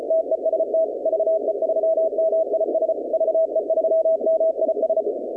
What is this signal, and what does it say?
Morse code, heard as a keyed tone, about 33 wpm, 660 Hz, with noise. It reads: TEST VE3MIS VE3MIS
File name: contests / 2019-cw-wpx-kt5j-k5pi- -2330 0-21